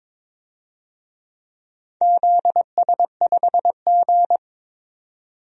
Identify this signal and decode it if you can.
Morse code, heard as a keyed tone, about 23 wpm, 700 Hz. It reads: ZS5G
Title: DXCC-100 Call Signs - 4 Characters - 22wpm